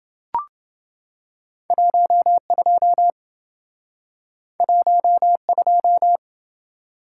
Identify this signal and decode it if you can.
Morse code, heard as a keyed tone, about 28 wpm, 700 Hz. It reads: 12 12